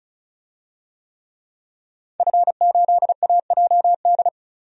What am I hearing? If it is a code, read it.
Morse code, heard as a keyed tone, about 34 wpm, 700 Hz. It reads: F8AJD
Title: DXCC-100 Call Signs - 5 Characters - 35wpm